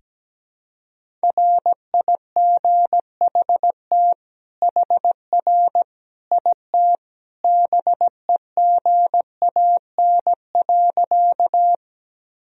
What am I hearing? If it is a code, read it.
Morse code, heard as a keyed tone, about 17 wpm, 700 Hz. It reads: RIGHT HR IT BEGAN.
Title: Sentences from Top 300 Words and Abbreviations 17wpm